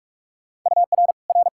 Morse code, heard as a keyed tone, about 46 wpm, 700 Hz. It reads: UR R
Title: K1USN SST QSO Practice 45wpm